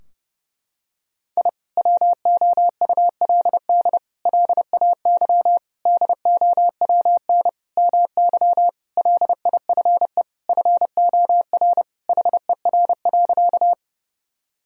Morse code, heard as a keyed tone, about 30 wpm, 700 Hz. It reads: I WOULD LAY DOWN MY LIFE FOR HER.
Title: Sentences from Top 1000 Words - Review Part 2 - 30wpm